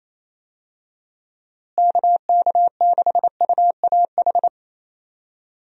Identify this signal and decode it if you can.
Morse code, heard as a keyed tone, about 28 wpm, 700 Hz. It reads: KK6UAH
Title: USA Call Signs - 6 Characters - Repeated 3x - 28wpm